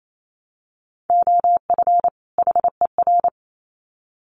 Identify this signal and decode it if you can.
Morse code, heard as a keyed tone, about 28 wpm, 700 Hz. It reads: OF HER
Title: Sets of 2 Words 28wpm